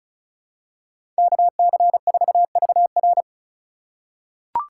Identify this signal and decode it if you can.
Morse code, heard as a keyed tone, about 34 wpm, 700 Hz. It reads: KC4VR
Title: USA Call Signs - 5 Characters - 3x Character Spacing - 35wpm